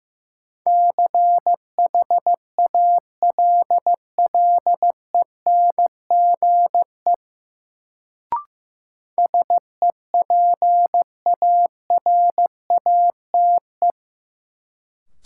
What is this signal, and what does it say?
Morse code, heard as a keyed tone, about 16 wpm, 700 Hz. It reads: CHALLENGE SEPARATE
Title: Sentences from Top 1000 Words - Review Part 7 - New Word Drill - 15wpm